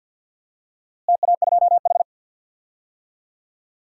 Morse code, heard as a keyed tone, about 52 wpm, 700 Hz. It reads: TA2H
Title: DXCC-100 Call Signs - 4 Characters - Repeated 2x - 50wpm